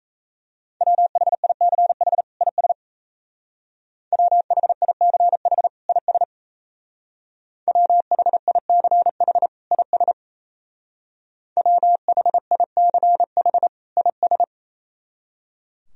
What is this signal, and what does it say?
Morse code, heard as a keyed tone, about 34 wpm, 700 Hz. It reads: WHICH IS WHICH IS WHICH IS WHICH IS
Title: Speed-Racing - Sets of 2 Words 28wpm